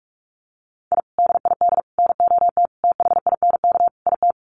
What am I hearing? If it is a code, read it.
Morse code, heard as a keyed tone, about 44 wpm, 700 Hz. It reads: I DID NOT THINK IT
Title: Sets of 5 Words Encore2 45wpm